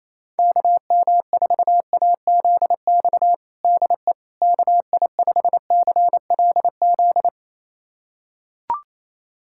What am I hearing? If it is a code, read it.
Morse code, heard as a keyed tone, about 28 wpm, 700 Hz. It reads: KM4AZX DE KI5CLZ
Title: USA Callsign DE Callsign - 6 Characters - 28wpm